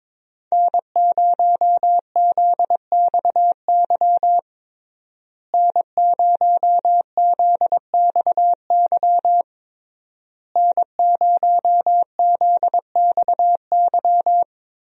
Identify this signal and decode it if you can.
Morse code, heard as a keyed tone, about 22 wpm, 700 Hz. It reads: N0ZXY N0ZXY N0ZXY
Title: USA Call Signs - 5 Characters - Repeated 3x - 22wpm